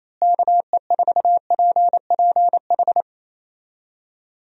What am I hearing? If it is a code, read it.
Morse code, heard as a keyed tone, about 28 wpm, 700 Hz. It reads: KE4PPH
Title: USA Call Signs - 6 Characters - Repeated 3x - 28wpm